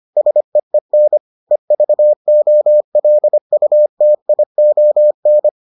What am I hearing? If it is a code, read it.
Morse code, heard as a keyed tone, about 25 wpm, 600 Hz. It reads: SEEN EVOLUTION